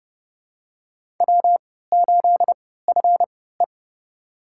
Morse code, heard as a keyed tone, about 29 wpm, spaced as for 17 wpm, 700 Hz. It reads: W8FE
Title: USA Call Signs - 4 Characters - 3x Character Spacing - 30wpm